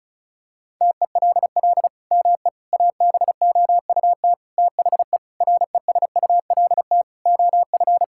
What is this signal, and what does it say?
Morse code, heard as a keyed tone, about 36 wpm, 700 Hz. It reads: TELL ME ABOUT THE RESULT OF
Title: Sentences from Top 1000 Words - Review Part 5 - 35wpm